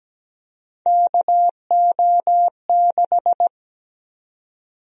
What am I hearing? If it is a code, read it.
Morse code, heard as a keyed tone, about 17 wpm, 700 Hz. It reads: KO6
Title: USA Call Sign Prefixes - Repeated 2x - 17wpm